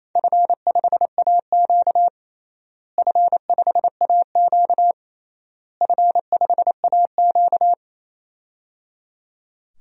Morse code, heard as a keyed tone, about 27 wpm, 700 Hz. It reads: F5AQ F5AQ F5AQ
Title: DXCC-100 Call Signs - 4 Characters - Repeated 3x - 28wpm